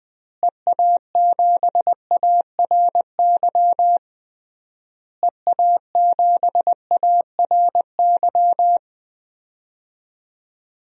Morse code, heard as a keyed tone, about 20 wpm, 700 Hz. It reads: EA7ARY EA7ARY
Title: DXCC-100 Call Signs - 6 Characters - Repeated 3x - 20wpm